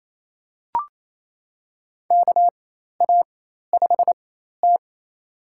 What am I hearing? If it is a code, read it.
Morse code, heard as a keyed tone, about 28 wpm, spaced as for 13 wpm, 700 Hz. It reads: KA5T
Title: USA Call Signs - 4 Characters - 4x Character Spacing - 28wpm